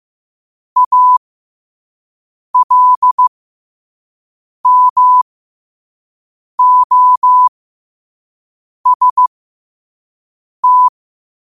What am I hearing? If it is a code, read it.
Morse code, heard as a keyed tone, about 15 wpm, spaced as for 5 wpm, 1000 Hz. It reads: ALMOST
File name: qotdfiles / 5wpm